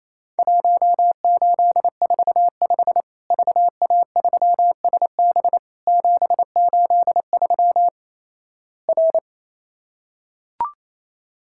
Morse code, heard as a keyed tone, about 28 wpm, 700 Hz. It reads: 1845 VA3SB 783 R
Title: Worked All Europe DX Contest - QTC Exchange - 28wpm